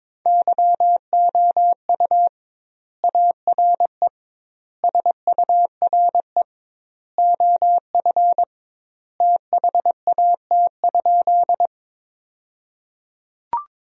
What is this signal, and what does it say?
Morse code, heard as a keyed tone, about 22 wpm, 700 Hz. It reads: YOU ARE SURE OF THAT?
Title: 2x Word-Spacing - Sentences from Top 300 Words - 22wpm